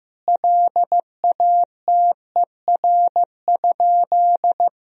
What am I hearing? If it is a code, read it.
Morse code, heard as a keyed tone, about 15 wpm, 700 Hz. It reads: LATER?
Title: Sentences from Top 1000 Words - Review Part 4 - 15wpm